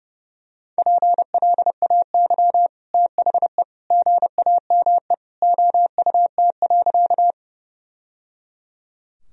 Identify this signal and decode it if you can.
Morse code, heard as a keyed tone, about 30 wpm, 700 Hz. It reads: PLAY THE GAME OUT.
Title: Sentences from Top 1000 Words - Review Part 3 - 30wpm